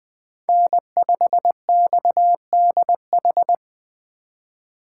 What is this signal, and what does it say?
Morse code, heard as a keyed tone, about 20 wpm, 700 Hz. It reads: N5XDH